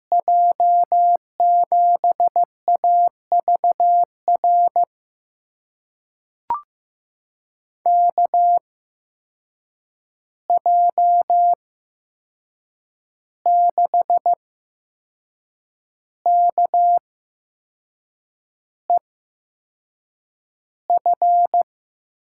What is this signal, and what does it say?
Morse code, heard as a keyed tone, about 15 wpm, spaced as for 6 wpm, 700 Hz. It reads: J7AVR KJ6KEF
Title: USA Call Signs - 6 Characters - 8x Character Spacing - 15wpm